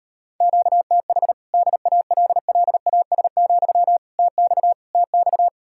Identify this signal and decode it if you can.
Morse code, heard as a keyed tone, about 38 wpm, 700 Hz. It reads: QTH DALLAS, TX TX